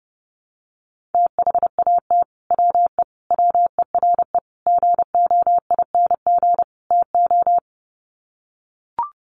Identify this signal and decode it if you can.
Morse code, heard as a keyed tone, about 30 wpm, 700 Hz. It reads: THAT WE WERE GOING TO
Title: Sets of 5 Words Encore2 30wpm